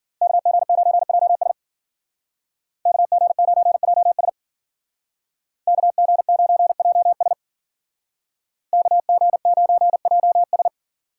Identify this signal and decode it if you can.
Morse code, heard as a keyed tone, about 52 wpm, 700 Hz. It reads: KG9JS KG9JS KG9JS KG9JS